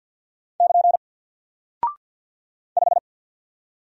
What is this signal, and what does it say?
Morse code, heard as a keyed tone, about 50 wpm, 700 Hz. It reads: <KN> 5
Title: Single Character (Letter:Number:Punctuation:Prosign) Alt 50wpm